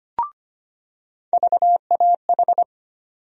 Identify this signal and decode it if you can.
Morse code, heard as a keyed tone, about 25 wpm, 700 Hz. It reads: VAH
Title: USA Call Sign Postfixes - 3 Letters 25wpm